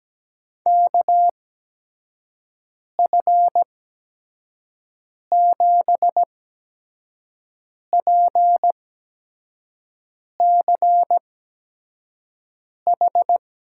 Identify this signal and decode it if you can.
Morse code, heard as a keyed tone, about 17 wpm, spaced as for 5 wpm, 700 Hz. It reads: KF7PCH
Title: USA Call Signs - 6 Characters - 8x Character Spacing - 17wpm